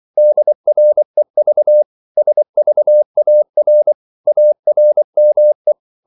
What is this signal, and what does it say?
Morse code, heard as a keyed tone, about 24 wpm, 600 Hz. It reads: DREV SVAR ARME
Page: Korte danske ord 24wpm